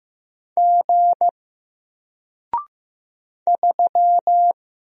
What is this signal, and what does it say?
Morse code, heard as a keyed tone, about 15 wpm, 700 Hz. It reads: G 3
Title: Single Letter-Number 15wpm